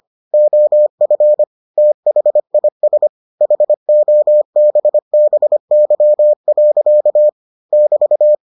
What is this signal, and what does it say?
Morse code, heard as a keyed tone, about 25 wpm, 600 Hz. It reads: OF THIS HOBBY. <BT>